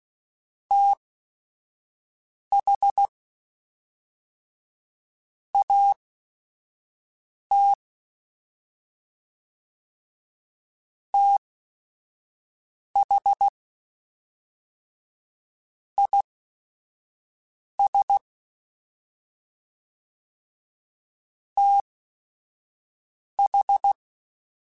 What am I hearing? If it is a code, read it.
Morse code, heard as a keyed tone, about 16 wpm, spaced as for 4 wpm, 800 Hz. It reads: THAT THIS TH